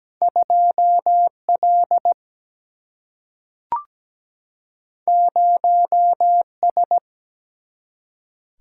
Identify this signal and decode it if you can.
Morse code, heard as a keyed tone, about 17 wpm, 700 Hz. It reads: 2L 0S
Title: Single-Letter Single-Number 17wpm